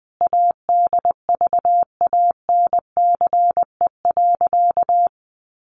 Morse code, heard as a keyed tone, about 20 wpm, 700 Hz. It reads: ADVANCE.
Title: Sentences from Top 1400 Words 20wpm